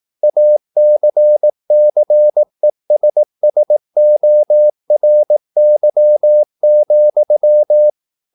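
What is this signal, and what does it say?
Morse code, heard as a keyed tone, about 18 wpm, 600 Hz. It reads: ACCESSORY,
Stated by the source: Morse practice at 18 wpm, English text, from CW med Gnister.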